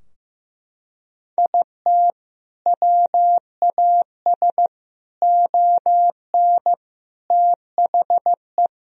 Morse code, heard as a keyed tone, about 16 wpm, 700 Hz. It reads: IT WAS ON THE
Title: Sentences from Top 1000 Words - Review Part 3 - 15wpm